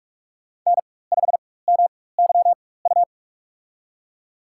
Morse code, heard as a keyed tone, about 46 wpm, spaced as for 21 wpm, 700 Hz. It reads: N5MYU
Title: USA Call Signs - 5 Characters - 4x Character Spacing - 45wpm